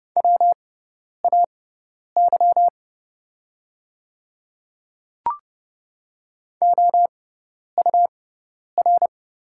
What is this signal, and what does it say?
Morse code, heard as a keyed tone, about 30 wpm, spaced as for 10 wpm, 700 Hz. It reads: WAY OUR
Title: 6x Character Spacing - Top 200 Words - 30wpm